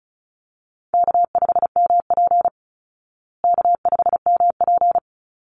Morse code, heard as a keyed tone, about 34 wpm, 700 Hz. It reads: K5MP K5MP
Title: Easy Call Signs Repeated 2x - 35wpm